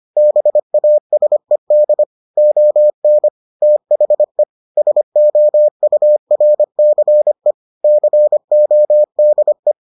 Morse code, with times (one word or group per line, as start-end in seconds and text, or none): 0.2-2.0 BASED
2.4-3.3 ON
3.6-4.4 THE
4.8-7.5 SOURCE
7.8-9.7 CODE